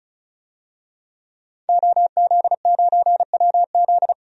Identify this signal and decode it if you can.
Morse code, heard as a keyed tone, about 36 wpm, 700 Hz. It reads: OZ9WZ
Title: DXCC-100 Call Signs - 5 Characters - 35wpm